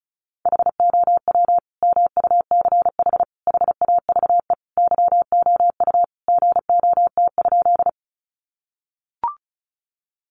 Morse code, heard as a keyed tone, about 34 wpm, 700 Hz. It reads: HOW MUCH HAVE YOU GOT?
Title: Sentences from Top 300 Words 35wpm